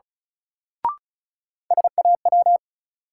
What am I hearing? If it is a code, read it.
Morse code, heard as a keyed tone, about 36 wpm, 700 Hz. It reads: SAW
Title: Sentences from Top 300 Words - New Word Drill - 35wpm